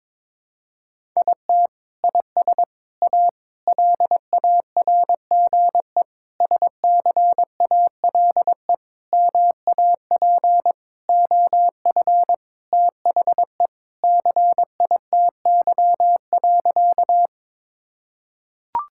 Morse code, heard as a keyed tone, about 22 wpm, 700 Hz. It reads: IT IS A LARGE SCALE MAP OF THE CITY.
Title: Sentences from Top 1000 Words - Review Part 5 - 22wpm